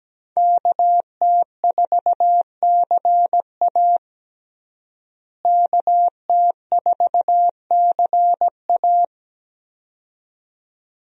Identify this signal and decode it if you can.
Morse code, heard as a keyed tone, about 17 wpm, 700 Hz. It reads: KT4CA KT4CA
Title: USA Call Signs - 5 Characters - Repeated 2x - 17wpm